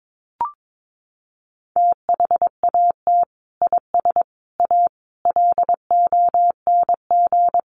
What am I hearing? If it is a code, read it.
Morse code, heard as a keyed tone, about 22 wpm, 700 Hz. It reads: THAT IS A LONG
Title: Sentences from Top 100 Words 22wpm